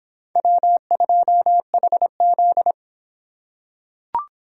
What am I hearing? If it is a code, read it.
Morse code, heard as a keyed tone, about 26 wpm, 700 Hz. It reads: W2HZ